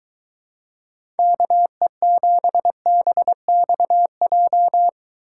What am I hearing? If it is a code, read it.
Morse code, heard as a keyed tone, about 22 wpm, 700 Hz. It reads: KE7BXJ